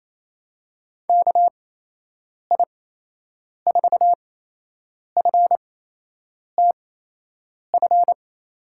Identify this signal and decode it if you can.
Morse code, heard as a keyed tone, about 28 wpm, spaced as for 8 wpm, 700 Hz. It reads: KI4FTF
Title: USA Call Signs - 8x Character Spacing - 28wpm